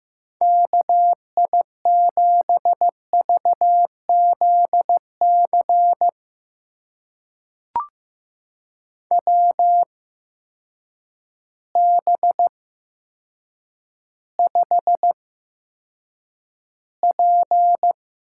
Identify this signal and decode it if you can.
Morse code, heard as a keyed tone, about 16 wpm, spaced as for 8 wpm, 700 Hz. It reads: KI7VZC WB5P
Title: USA Call Signs - 6 Characters - 8x Character Spacing - 15wpm